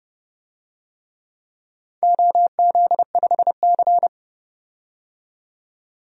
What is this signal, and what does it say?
Morse code, heard as a keyed tone, about 32 wpm, 700 Hz. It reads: OZ5C
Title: DXCC-100 Call Signs - 4 Characters - 30wpm